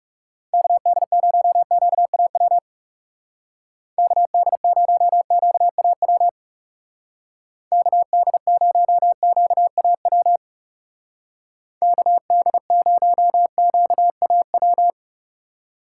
Morse code, heard as a keyed tone, about 38 wpm, 700 Hz. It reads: KD0QAW KD0QAW KD0QAW KD0QAW